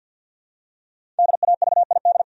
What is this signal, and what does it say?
Morse code, heard as a keyed tone, about 52 wpm, 700 Hz. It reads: DAVID
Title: USA Common Names - 50wpm